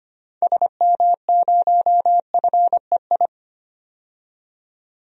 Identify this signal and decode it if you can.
Morse code, heard as a keyed tone, about 25 wpm, 700 Hz. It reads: SM0FEI
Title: DXCC-100 Call Signs - 6 Characters - 25wpm